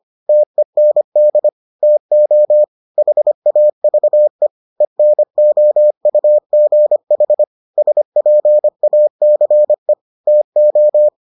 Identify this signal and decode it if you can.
Morse code, heard as a keyed tone, about 25 wpm, 600 Hz. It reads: TEND TO HAVE ENOUGH SPACE TO